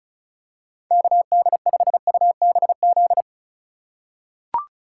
Morse code, heard as a keyed tone, about 36 wpm, 700 Hz. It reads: KD5UBZ
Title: USA Call Signs - 6 Characters - 3x Character Spacing - 35wpm